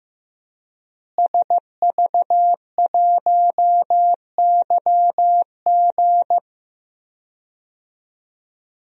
Morse code, heard as a keyed tone, about 15 wpm, 700 Hz. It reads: SV1YG
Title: DXCC-100 Call Signs - 5 Characters - Repeated 2x - 15wpm